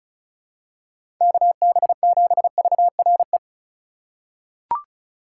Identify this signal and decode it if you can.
Morse code, heard as a keyed tone, about 34 wpm, 700 Hz. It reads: KB7VRE
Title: USA Call Signs - 4x Character Spacing - 35wpm